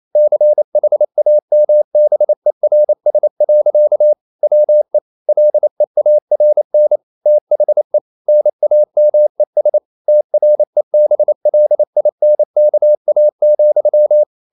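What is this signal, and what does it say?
Morse code, heard as a keyed tone, about 28 wpm, 600 Hz. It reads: CHAMBERS. WE LEARN THE NAMES TREBLINKA,
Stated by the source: Morse code practice at 28 wpm, text of Ken Follett's novel Never in English, from CW med Gnister.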